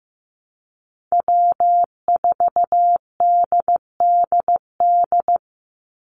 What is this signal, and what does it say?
Morse code, heard as a keyed tone, about 15 wpm, 700 Hz. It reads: W4DDD
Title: Call Signs Repeated 3x - 15wpm